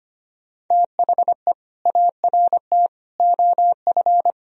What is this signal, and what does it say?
Morse code, heard as a keyed tone, about 25 wpm, 700 Hz. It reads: THE ART OF
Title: Sentences from Top 1000 Words - Review Part 3 - 25wpm